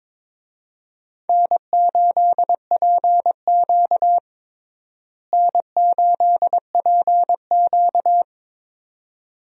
Morse code, heard as a keyed tone, about 22 wpm, 700 Hz. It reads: N8PQ N8PQ
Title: USA Call Signs - 4 Characters - Repeated 2x - 22wpm